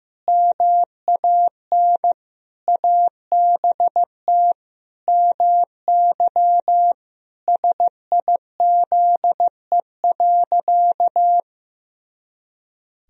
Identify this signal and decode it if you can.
Morse code, heard as a keyed tone, about 15 wpm, 700 Hz. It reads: MAN ABT MY SIZE.
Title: Sentences from Top 400 Words and Abbreviations 15wpm